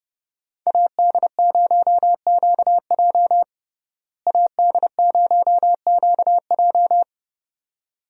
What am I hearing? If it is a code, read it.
Morse code, heard as a keyed tone, about 30 wpm, 700 Hz. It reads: AD0QJ AD0QJ
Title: USA Call Signs - 5 Characters - Repeated 2x - 30wpm